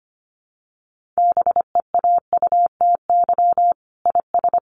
Morse code, heard as a keyed tone, about 25 wpm, 700 Hz. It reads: BEAUTY IS
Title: Sentences from Top 1300 Words 25wpm